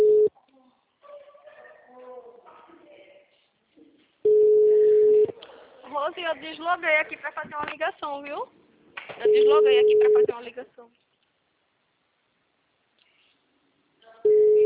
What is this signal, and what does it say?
Ringback tone (Europe, 425 Hz), with speech.